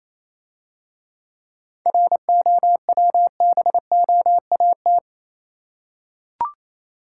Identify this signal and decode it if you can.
Morse code, heard as a keyed tone, about 28 wpm, 690 Hz. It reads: ROWBOAT
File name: Easy Words - Speed-Racing with Hints - 22wpm